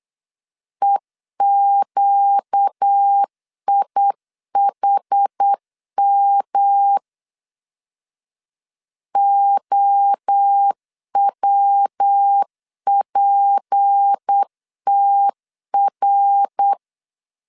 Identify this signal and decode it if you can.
Morse code, heard as a keyed tone, about 8 wpm, 800 Hz. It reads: EQIHM OWPTR
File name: LEZIONI / 6 PKX0 / 6 P 8wpm 3025